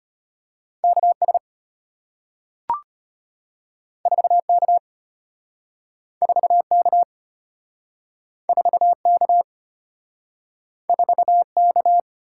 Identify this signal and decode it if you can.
Morse code, heard as a keyed tone, about 34 wpm, 700 Hz. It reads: KS 4K 4K 4K 4K